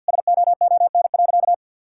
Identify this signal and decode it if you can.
Morse code, heard as a keyed tone, about 50 wpm, 700 Hz. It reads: SOON.
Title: Sentences from Top 1600 Words 50wpm